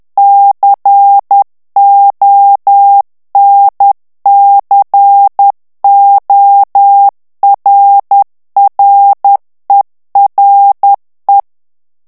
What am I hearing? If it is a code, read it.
Morse code, heard as a keyed tone, about 10 wpm, 800 Hz. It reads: CONCORRERE
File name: ITALIEN texte clair 2 vitesse 10 mots minute